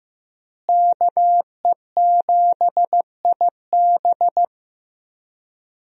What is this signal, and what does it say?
Morse code, heard as a keyed tone, about 15 wpm, 700 Hz. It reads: KE7IB
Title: USA Call Signs - 5 Characters - Repeated 3x - 15wpm